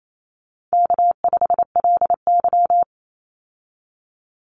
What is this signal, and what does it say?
Morse code, heard as a keyed tone, about 28 wpm, 700 Hz. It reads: K5LY
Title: Easy Call Signs Repeated 3x - 28wpm